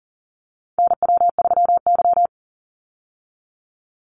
Morse code, heard as a keyed tone, about 40 wpm, 700 Hz. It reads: NW3Y
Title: Call Signs Repeated 2x - 40wpm